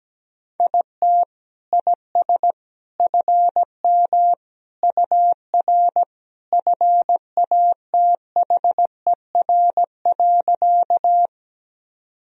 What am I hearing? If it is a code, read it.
Morse code, heard as a keyed tone, about 17 wpm, 700 Hz. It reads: IT IS FM UR FATHER.
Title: Sentences from Top 300 Words and Abbreviations 17wpm